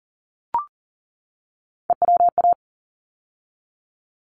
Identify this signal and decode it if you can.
Morse code, heard as a keyed tone, about 38 wpm, 700 Hz. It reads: EWA
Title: ARRL Sections 40wpm